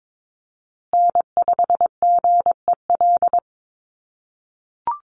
Morse code, heard as a keyed tone, about 22 wpm, 700 Hz. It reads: N5GEL
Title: USA Call Signs Repeated 2x 22wpm